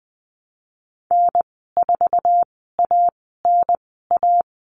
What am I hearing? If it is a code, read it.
Morse code, heard as a keyed tone, about 20 wpm, spaced as for 14 wpm, 700 Hz. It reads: N4ANA
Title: 062 Lesson 11g - Letter h - Call Signs 20wpm